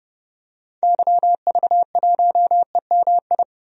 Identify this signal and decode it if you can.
Morse code, heard as a keyed tone, about 31 wpm, 700 Hz. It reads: YV1EMI
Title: DXCC-100 Call Signs - 6 Characters - 30wpm